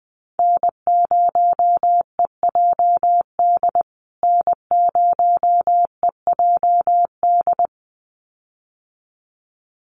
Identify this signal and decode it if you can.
Morse code, heard as a keyed tone, about 20 wpm, 700 Hz. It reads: N0EJD N0EJD
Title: USA Call Signs Repeated 2x 20wpm